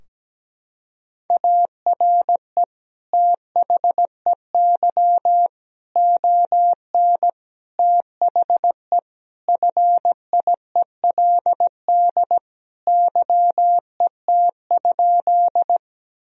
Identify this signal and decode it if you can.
Morse code, heard as a keyed tone, about 17 wpm, 700 Hz. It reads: ARE THEY ON THE FIELD YET?
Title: Sentences from Top 1000 Words - Review Part 2 - 17wpm